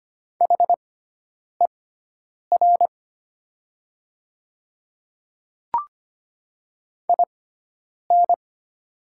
Morse code, heard as a keyed tone, about 25 wpm, spaced as for 9 wpm, 700 Hz. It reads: HER IN
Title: 6x Character Spacing - Top 100 Words - 25wpm